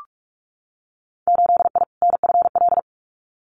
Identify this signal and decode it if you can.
Morse code, heard as a keyed tone, about 46 wpm, 700 Hz. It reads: 8I NFL
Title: Winter Field Day Exchanges Encore 45wpm